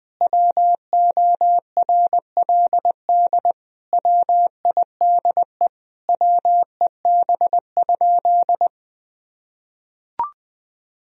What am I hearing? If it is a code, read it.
Morse code, heard as a keyed tone, about 20 wpm, 700 Hz. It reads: WORLD WIDE WEB?
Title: Sentences from Top 1000 Words - Review Part 4 - 20wpm